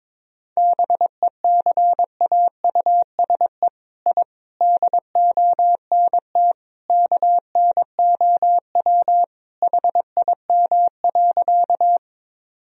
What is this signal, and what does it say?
Morse code, heard as a keyed tone, about 21 wpm, 700 Hz. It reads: BECAUSE I DONT KNOW HIM.
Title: Sentences from Top 1000 Words - Review Part 3 - 22wpm